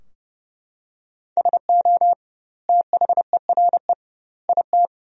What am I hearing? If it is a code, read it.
Morse code, heard as a keyed tone, about 30 wpm, 700 Hz. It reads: SO THERE IT
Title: 2x Word-Spacing - Sentences from Top 100 Words - 30wpm